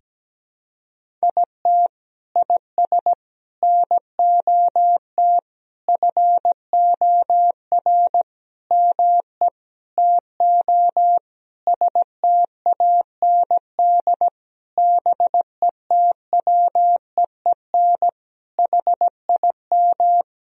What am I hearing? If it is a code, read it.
Morse code, heard as a keyed tone, about 17 wpm, 700 Hz. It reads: IT IS NOT FOR ME TO STAND BETWEEN HIM
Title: Sentences from Top 1000 Words - Review Part 1 - 17wpm